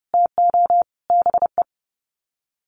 Morse code, heard as a keyed tone, about 30 wpm, 700 Hz. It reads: TO BE
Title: Sets of 5 Words 30wpm